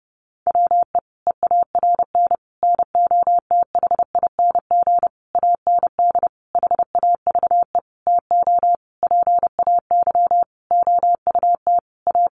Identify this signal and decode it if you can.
Morse code, heard as a keyed tone, about 30 wpm, 700 Hz. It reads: WE EARN NOTHING AND HAVE TO PAY OUT A
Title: Sentences from Top 1500 Words 30wpm